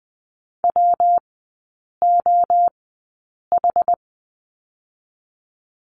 Morse code, heard as a keyed tone, about 20 wpm, 700 Hz. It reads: W O H
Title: 114 Lesson 19e - Letter w - Three Characters 20wpm